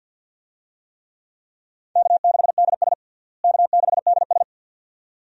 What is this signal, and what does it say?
Morse code, heard as a keyed tone, about 48 wpm, 700 Hz. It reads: K6DS K6DS
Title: USA Call Signs - 4 Characters - Repeated 2x - 50wpm